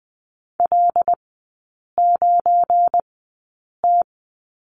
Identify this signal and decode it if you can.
Morse code, heard as a keyed tone, about 20 wpm, spaced as for 8 wpm, 700 Hz. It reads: L9T
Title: 184 Lesson 30e - Letter b - Three Characters 20wpm